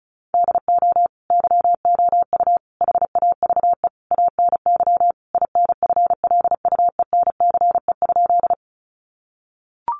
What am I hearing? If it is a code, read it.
Morse code, heard as a keyed tone, about 34 wpm, 700 Hz. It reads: DO YOU HAVE ANY INFLUENCE?
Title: Sentences from Top 1100 Words 35wpm